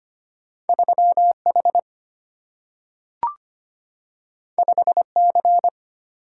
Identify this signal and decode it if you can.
Morse code, heard as a keyed tone, about 25 wpm, 700 Hz. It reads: 3 H 5 C
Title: Single-Letter Single-Number 25wpm